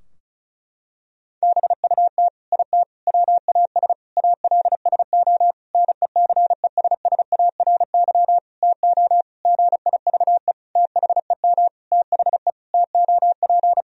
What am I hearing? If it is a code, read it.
Morse code, heard as a keyed tone, about 36 wpm, 700 Hz. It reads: BUT IT WAS ALSO NECESSARY TO GIVE THEM THE TOP
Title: Sentences from Top 1000 Words - Review Part 2 - 35wpm